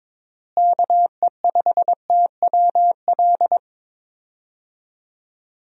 Morse code, heard as a keyed tone, about 22 wpm, 700 Hz. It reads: KE5TWL
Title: USA Call Signs - 6 Characters - Repeated 3x - 22wpm